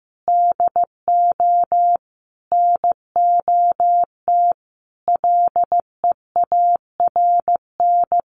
Morse code, heard as a keyed tone, about 15 wpm, 700 Hz. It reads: DO NOT LEARN
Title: Sentences from Top 1400 Words 15wpm